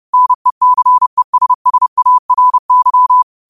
Morse code, heard as a keyed tone, about 30 wpm, 1000 Hz. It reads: NECESSARY